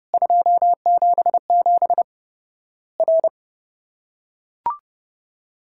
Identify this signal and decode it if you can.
Morse code, heard as a keyed tone, about 30 wpm, 700 Hz. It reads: 277 R
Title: Worked All Europe DX Contest - QTC Exchange - 30wpm